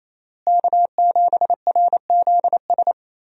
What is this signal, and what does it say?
Morse code, heard as a keyed tone, about 28 wpm, 700 Hz. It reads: K7RZS
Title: USA Call Signs - 5 Characters - Repeated 3x - 28wpm